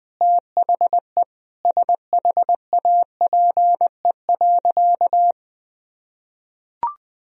Morse code, heard as a keyed tone, about 20 wpm, 700 Hz. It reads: THE SHAPE.
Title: Sentences from Top 1000 Words - Review Part 3 - 20wpm